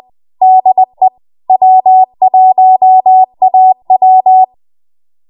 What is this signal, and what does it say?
Morse code, heard as a keyed tone, about 20 wpm, 750 Hz. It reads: DE W1AW
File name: Morse / 20WPM